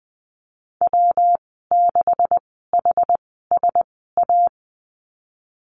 Morse code, heard as a keyed tone, about 20 wpm, spaced as for 15 wpm, 700 Hz. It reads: W6HSA
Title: 167 Lesson 27g - Number 7 - Call Signs 20wpm